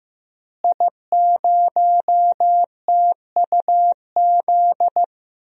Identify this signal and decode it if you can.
Morse code, heard as a keyed tone, about 15 wpm, 700 Hz. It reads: I0TUZ
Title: DXCC-100 Call Signs - 5 Characters - Repeated 3x - 15wpm